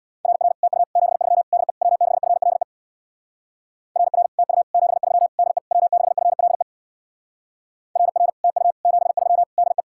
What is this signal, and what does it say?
Morse code, heard as a keyed tone, about 62 wpm, 700 Hz. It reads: RR TU 73 DE W6FLE RR TU 73 DE W6FLE RR TU 73 DE
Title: Speed-Racing Common QSO Phrases - 50wpm